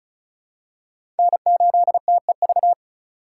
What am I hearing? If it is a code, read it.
Morse code, heard as a keyed tone, about 36 wpm, 700 Hz. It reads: N8TEV
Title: USA Call Signs - 3x Character Spacing - 35wpm